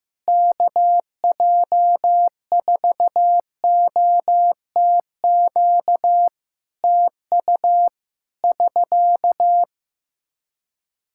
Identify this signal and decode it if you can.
Morse code, heard as a keyed tone, about 15 wpm, 700 Hz. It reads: KJ4OTQ TU <SK>
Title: Practice QSO - Exchange 4 - Ending - 15wpm